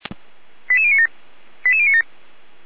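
Five-tone selective call (ZVEI), heard as quick runs of five tones, tones of about 70 ms. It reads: 88097 88097
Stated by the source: Five-tone ZVEI selective call received on SDR.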